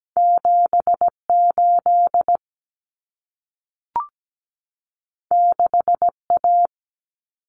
Morse code, heard as a keyed tone, about 17 wpm, 700 Hz. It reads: 78 6A
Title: Common Cut Numbers - 2-Digit 17wpm